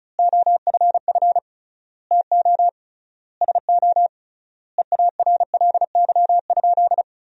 Morse code, heard as a keyed tone, about 34 wpm, 700 Hz. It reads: OFF TO SO EARLY?
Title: 3x Word-Spacing - Sentences from Top 400 Words - 35wpm